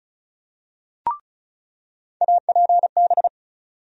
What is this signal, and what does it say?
Morse code, heard as a keyed tone, about 36 wpm, 700 Hz. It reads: APB
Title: USA Call Sign Postfixes - 3 Letters 35wpm